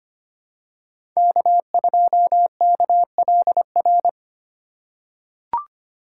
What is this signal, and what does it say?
Morse code, heard as a keyed tone, about 25 wpm, 700 Hz. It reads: K2KLR
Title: USA Call Signs - 5 Characters - 6x Character Spacing - 25wpm